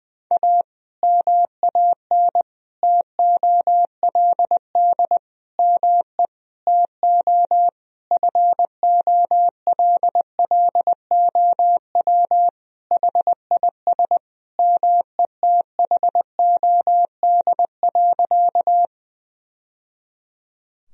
Morse code, heard as a keyed tone, about 20 wpm, 700 Hz. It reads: A MAN TOLD ME TO FOLLOW HIS METHOD.
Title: Sentences from Top 1000 Words - Review Part 3 - 20wpm